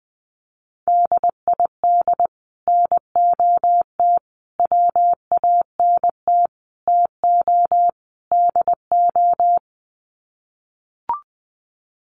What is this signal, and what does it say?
Morse code, heard as a keyed tone, about 20 wpm, 700 Hz. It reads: DID NOT WANT TO DO
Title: Sets of 5 Words Encore 20wpm